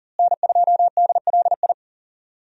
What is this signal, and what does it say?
Morse code, heard as a keyed tone, about 37 wpm, 700 Hz. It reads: N2DLI
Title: USA Call Signs - 5 Characters - Repeated 2x - 40wpm